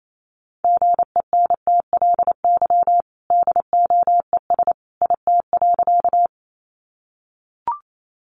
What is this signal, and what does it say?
Morse code, heard as a keyed tone, about 28 wpm, 700 Hz. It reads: GENTLY DOES IT.
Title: Sentences from Top 1500 Words 28wpm